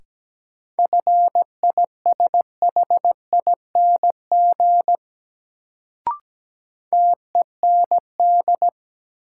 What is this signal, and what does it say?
Morse code, heard as a keyed tone, about 17 wpm, 700 Hz. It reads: FISHING TEND
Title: Sentences from Top 1000 Words - Review Part 5 - New Word Drill - 17wpm